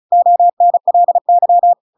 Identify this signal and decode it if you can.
Morse code, heard as a keyed tone, about 34 wpm, 700 Hz. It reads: ONLY
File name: Pinocchio / Pinocchio ch7 35 35 ik1hge